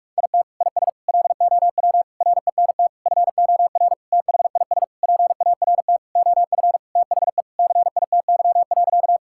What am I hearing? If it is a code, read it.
Morse code, heard as a keyed tone, about 46 wpm, 700 Hz. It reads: IT IS LOW RENT FOR THIS PART OF THE CITY.
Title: Sentences from Top 1000 Words - Review Part 5 - 45wpm